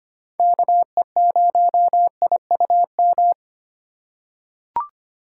Morse code, heard as a keyed tone, about 25 wpm, 700 Hz. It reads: KE0IUM
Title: USA Call Signs - 3x Character Spacing - 25wpm